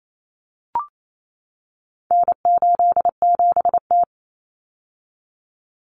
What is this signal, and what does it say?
Morse code, heard as a keyed tone, about 28 wpm, 700 Hz. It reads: N87T
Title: Common Cut Numbers - 4-Digit 28wpm